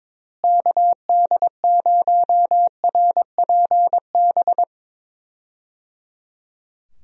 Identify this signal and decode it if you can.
Morse code, heard as a keyed tone, about 22 wpm, 700 Hz. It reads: KD0RPB
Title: USA Call Signs - 6 Characters - Repeated 3x - 22wpm